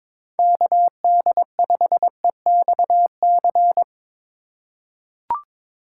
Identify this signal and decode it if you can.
Morse code, heard as a keyed tone, about 22 wpm, 700 Hz. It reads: KD5EXC
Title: USA Call Signs - 6 Characters - 2x Character Spacing - 22wpm